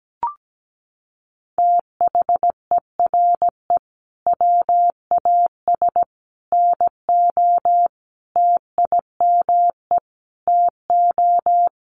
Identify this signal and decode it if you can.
Morse code, heard as a keyed tone, about 17 wpm, 700 Hz. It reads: THERE WAS NO TIME TO
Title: Sentences from Top 1100 Words 17wpm